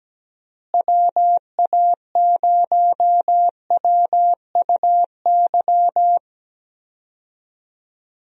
Morse code, heard as a keyed tone, about 17 wpm, 700 Hz. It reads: WA0WUY
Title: USA Call Signs - 6 Characters - Repeated 3x - 17wpm